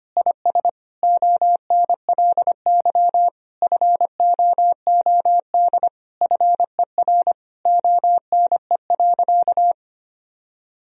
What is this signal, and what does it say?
Morse code, heard as a keyed tone, about 25 wpm, 700 Hz. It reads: IS ONLY FOOD FER ONE.
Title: Sentences from Top 300 Words and Abbreviations 25wpm